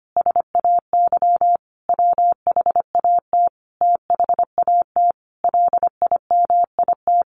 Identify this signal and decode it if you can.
Morse code, heard as a keyed tone, about 25 wpm, 700 Hz. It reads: SAY WHAT THAT LIMIT
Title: Sentences from Top 1500 Words 25wpm